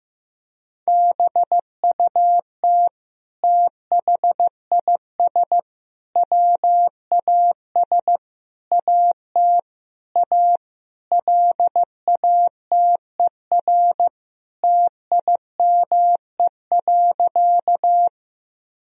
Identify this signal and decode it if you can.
Morse code, heard as a keyed tone, about 16 wpm, 700 Hz. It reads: BUT THIS WAS AT A LATER TIME.
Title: Sentences from Top 1000 Words - Review Part 4 - 15wpm